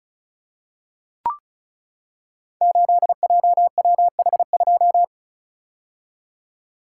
Morse code, heard as a keyed tone, about 36 wpm, 690 Hz. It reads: 8JWH2
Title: Code Groups - 5 Letter:Numbers - 35wpm